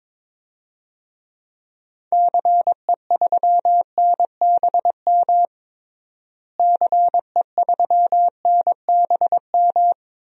Morse code, heard as a keyed tone, about 22 wpm, 700 Hz. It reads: CE3NBM CE3NBM
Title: DXCC-100 Call Signs - 6 Characters - Repeated 3x - 22wpm